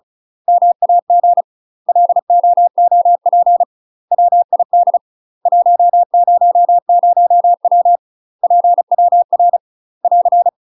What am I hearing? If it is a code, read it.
Morse code, heard as a keyed tone, about 36 wpm, 700 Hz. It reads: MAG LOOP WID 100W PWR <AR>
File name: UK MASTER QSOs TRAINING / 35WPM / UK MASTER QSOs 0035